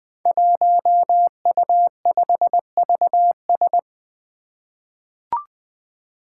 Morse code, heard as a keyed tone, about 20 wpm, 700 Hz. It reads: 1U5VS
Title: Five Challenging Letter-Number Combinations 20wpm